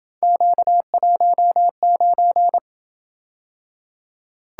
Morse code, heard as a keyed tone, about 28 wpm, 700 Hz. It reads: Q19